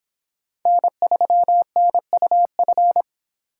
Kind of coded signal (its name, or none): Morse code